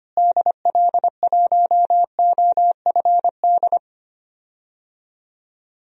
Morse code, heard as a keyed tone, about 25 wpm, 700 Hz. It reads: DL1OFD
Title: DXCC-100 Call Signs - 6 Characters - Repeated 3x - 25wpm